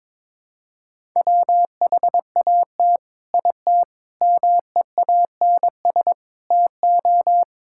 Morse code, heard as a keyed tone, about 21 wpm, 700 Hz. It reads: WHAT IT MEANS TO
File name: practice / 22wpm / Sets of 6 Words - 22wpm